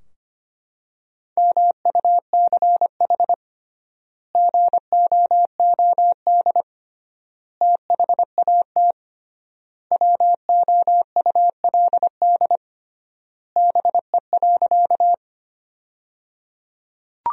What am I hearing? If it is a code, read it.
Morse code, heard as a keyed tone, about 25 wpm, 700 Hz. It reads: MUCH GOOD THAT WOULD BE.
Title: 3x Word-Spacing - Sentences from Top 200 Words - 25wpm